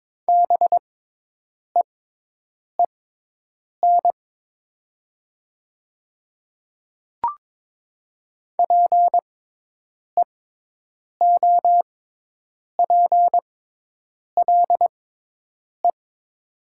Morse code, heard as a keyed tone, about 22 wpm, spaced as for 8 wpm, 700 Hz. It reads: BEEN PEOPLE
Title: 6x Character Spacing - Top 100 Words - 22wpm